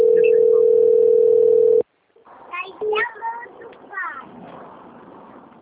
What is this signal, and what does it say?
Ringback tone (North America), 440 + 480 Hz, with speech.
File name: rg-1600-unknown- - - .7526